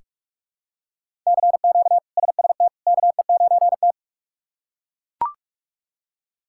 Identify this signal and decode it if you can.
Morse code, heard as a keyed tone, about 42 wpm, 700 Hz. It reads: CQ SST KE9T
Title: K1USN SST QSO Practice - 45wpm